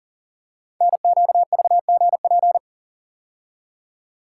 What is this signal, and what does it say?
Morse code, heard as a keyed tone, about 40 wpm, 700 Hz. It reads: NQVGP
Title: Code Groups - 5 Letters - 40wpm